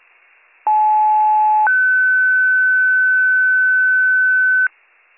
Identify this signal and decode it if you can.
Two-tone page, heard as a one-second tone, then a three-second tone, about 860 then 1500 Hz, with noise. It is heard from 0.7 to 4.7 s.